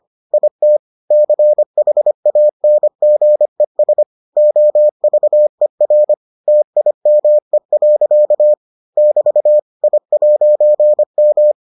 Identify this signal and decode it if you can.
Morse code, heard as a keyed tone, about 25 wpm, 600 Hz. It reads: IT CHANGES OVER TIME. <BT> I'M